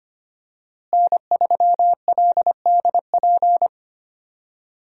Morse code, heard as a keyed tone, about 25 wpm, 700 Hz. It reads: N3LDP
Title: USA Call Signs - 5 Characters - Repeated 3x - 25wpm